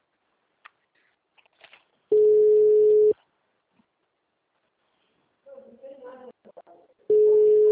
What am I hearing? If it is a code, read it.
Ringback tone (Europe, 425 Hz), with speech.
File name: out- -2005- - - .22506